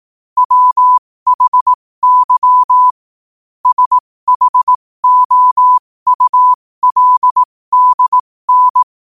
Morse code, heard as a keyed tone, about 18 wpm, spaced as for 15 wpm, 1000 Hz. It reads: WHY SHOULDN
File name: 15wpm